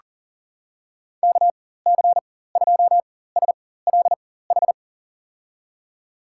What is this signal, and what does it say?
Morse code, heard as a keyed tone, about 40 wpm, spaced as for 19 wpm, 700 Hz. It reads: KC2SLH
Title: USA Call Signs - 6 Characters - 4x Character Spacing - 40wpm